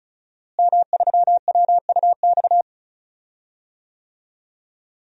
Morse code, heard as a keyed tone, about 34 wpm, 700 Hz. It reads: M3WUX
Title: DXCC-100 Call Signs - 5 Characters - Repeated 3x - 35wpm